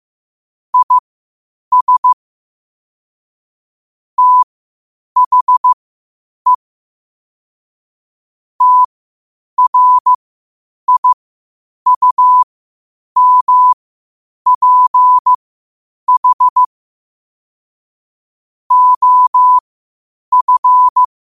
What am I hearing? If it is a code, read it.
Morse code, heard as a keyed tone, about 15 wpm, spaced as for 8 wpm, 1000 Hz. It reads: IS THE TRIUMPH OF
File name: qotdfiles / 8wpm